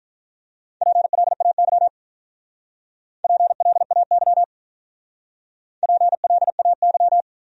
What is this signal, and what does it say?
Morse code, heard as a keyed tone, about 42 wpm, 700 Hz. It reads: PLAY PLAY PLAY
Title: Speed-Racing Top 200 Words 35wpm